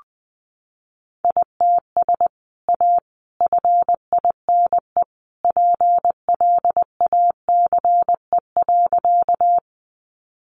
Morse code, heard as a keyed tone, about 19 wpm, 700 Hz. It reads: ITS A FINE PLACE.
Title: Sentences from Top 600 Words 20wpm